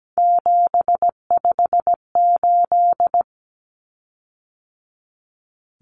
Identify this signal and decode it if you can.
Morse code, heard as a keyed tone, about 17 wpm, 700 Hz. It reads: 758
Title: Common Cut Numbers - 3-Digit 17wpm